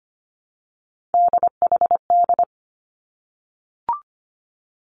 Morse code, heard as a keyed tone, about 25 wpm, 700 Hz. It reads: DHD
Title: Three Challenging Letter-Number Combinations 25wpm